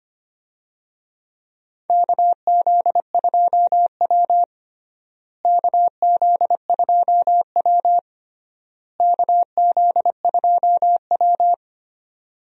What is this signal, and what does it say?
Morse code, heard as a keyed tone, about 25 wpm, 700 Hz. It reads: KZ2W KZ2W KZ2W
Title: USA Call Signs - 4 Characters - Repeated 3x - 25wpm